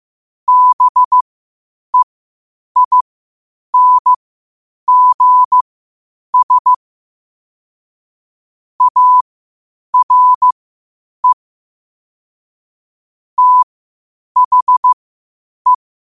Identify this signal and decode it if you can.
Morse code, heard as a keyed tone, about 15 wpm, spaced as for 8 wpm, 1000 Hz. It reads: BEINGS ARE THE
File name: qotdfiles / 8wpm